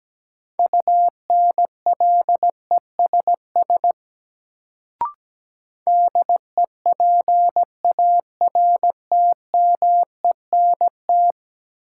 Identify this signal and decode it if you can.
Morse code, heard as a keyed tone, about 17 wpm, 700 Hz. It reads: UNLESS DEPARTMENT
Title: Sentences from Top 1000 Words - Review Part 5 - New Word Drill - 17wpm